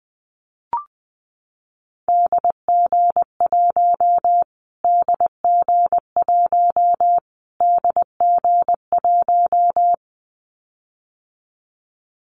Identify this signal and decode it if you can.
Morse code, heard as a keyed tone, about 20 wpm, 700 Hz. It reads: DG1 DG1 DG1
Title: Callsign Prefixes for Top 25 DXCC - Repeated 3x - 20wpm